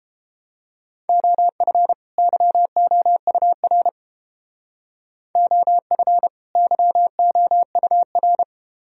Morse code, heard as a keyed tone, about 31 wpm, 700 Hz. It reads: OF YOUR OF YOUR